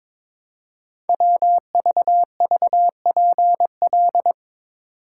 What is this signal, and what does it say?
Morse code, heard as a keyed tone, about 22 wpm, 700 Hz. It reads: WVVPL
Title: Code Groups - 5 Letter:Numbers - 22wpm